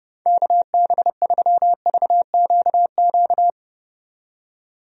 Morse code, heard as a keyed tone, about 30 wpm, 700 Hz. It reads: KB3VQQ
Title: USA Call Signs - 6 Characters - 30wpm